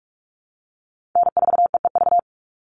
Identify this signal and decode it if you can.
Morse code, heard as a keyed tone, about 44 wpm, 700 Hz. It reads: N4EEV
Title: Call Signs Repeated 3x - 45wpm